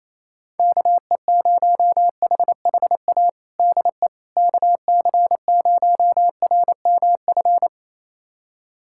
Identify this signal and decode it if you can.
Morse code, heard as a keyed tone, about 28 wpm, 700 Hz. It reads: KE0HHA DE KC0RMF
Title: USA Callsign DE Callsign 28wpm